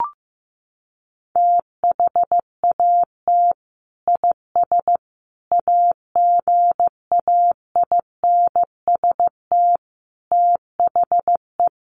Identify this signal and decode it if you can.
Morse code, heard as a keyed tone, about 15 wpm, 700 Hz. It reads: THAT IS AGAINST THE
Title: Sentences from Top 1500 Words 15wpm